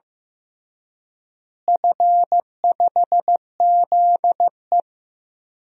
Morse code, heard as a keyed tone, about 15 wpm, 700 Hz. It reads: F5ZE
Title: DXCC-100 Call Signs - 4 Characters - Repeated 3x - 15wpm